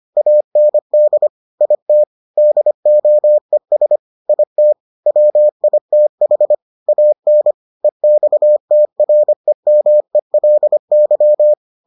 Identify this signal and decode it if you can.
Morse code, heard as a keyed tone, about 25 wpm, 600 Hz. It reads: AND IT DOES IT WITH AN EXTREMELY